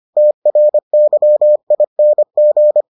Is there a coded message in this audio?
Yes: Morse code from 0.2 to 2.8 s, heard as a keyed tone, about 25 wpm, 600 Hz. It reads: TRYING